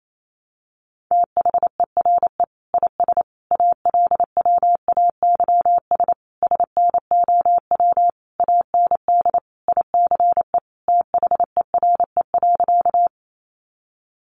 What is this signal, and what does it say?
Morse code, heard as a keyed tone, about 28 wpm, 700 Hz. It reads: THERE IS ALWAYS SNOW AND ICE THERE.
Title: Sentences from Top 1000 Words 28wpm